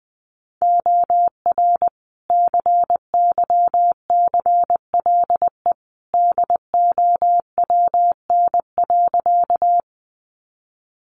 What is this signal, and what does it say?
Morse code, heard as a keyed tone, about 20 wpm, 700 Hz. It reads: OR CYCLE DOWN.
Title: Sentences from Top 900 Words 20wpm